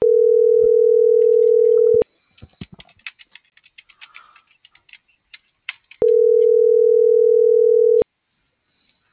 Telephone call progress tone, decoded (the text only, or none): ringback tone